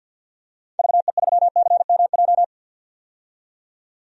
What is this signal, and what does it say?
Morse code, heard as a keyed tone, about 52 wpm, 700 Hz. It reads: VE3CMJ
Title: DXCC-100 Call Signs - 6 Characters - 50wpm